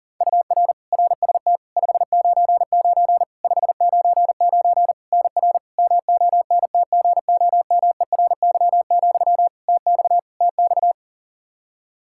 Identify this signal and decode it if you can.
Morse code, heard as a keyed tone, about 37 wpm, 700 Hz. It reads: UR RST 599 599 NR MONTGOMERY, TX TX
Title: Practice QSO - Exchange 1 - Beginning - 40wpm